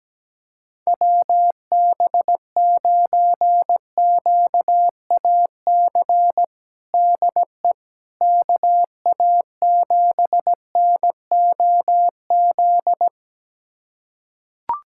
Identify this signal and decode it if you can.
Morse code, heard as a keyed tone, about 17 wpm, 700 Hz. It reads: WB9QAC DE KA7NOZ
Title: USA Callsign DE Callsign - 6 Characters - 17wpm